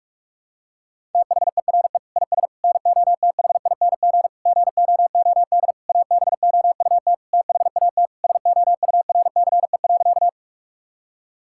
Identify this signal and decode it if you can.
Morse code, heard as a keyed tone, about 46 wpm, 700 Hz. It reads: THERE IS NOTHING GOOD ABOUT THAT SOURCE.
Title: Sentences from Top 1000 Words - Review Part 5 - 45wpm